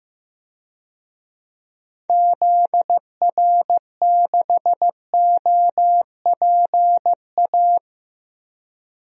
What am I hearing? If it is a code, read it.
Morse code, heard as a keyed tone, about 15 wpm, 700 Hz. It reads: ZR6OPA
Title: DXCC-100 Call Signs - 6 Characters - Repeated 2x - 15wpm